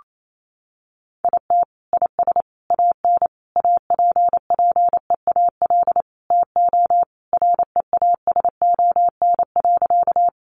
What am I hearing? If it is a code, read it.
Morse code, heard as a keyed tone, about 28 wpm, 700 Hz. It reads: IT IS AN APPEAL TO REASON.
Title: Sentences from Top 1300 Words 28wpm